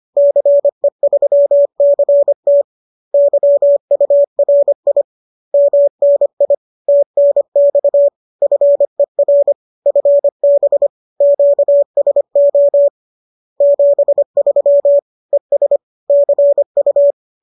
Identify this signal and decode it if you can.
Morse code, heard as a keyed tone, about 25 wpm, 570 Hz. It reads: CE3CT YURI MNI TNX FER FB QSO 73 ES CU